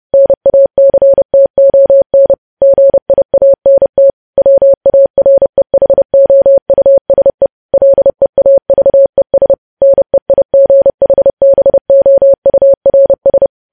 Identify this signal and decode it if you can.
Morse code, heard as a keyed tone, about 30 wpm, 550 Hz. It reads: NACTON GIANT WAREHOUSE LEAVES NEIGHBOURS
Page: news headlines in morse code at 30 wpm